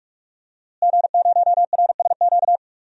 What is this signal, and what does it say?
Morse code, heard as a keyed tone, about 43 wpm, 700 Hz. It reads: G0RSQ
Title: DXCC-100 Call Signs - 5 Characters - Repeated 3x - 45wpm